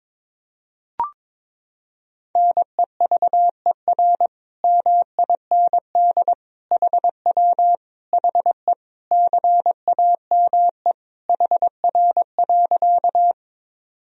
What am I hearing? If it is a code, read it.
Morse code, heard as a keyed tone, about 22 wpm, 700 Hz. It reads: NEVER MIND HW HE CAME HR.
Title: Sentences from Top 500 Words and Abbreviations 22wpm